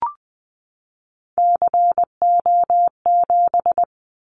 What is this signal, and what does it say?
Morse code, heard as a keyed tone, about 20 wpm, 700 Hz. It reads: CO7
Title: Callsign Prefixes for Top 100 DXCC - 20wpm